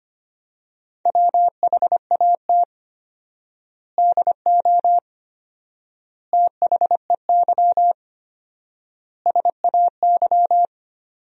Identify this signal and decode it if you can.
Morse code, heard as a keyed tone, about 25 wpm, 700 Hz. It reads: WHAT DO THEY SAY
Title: 4x Word-Spacing - Sentences from Top 200 Words - 25wpm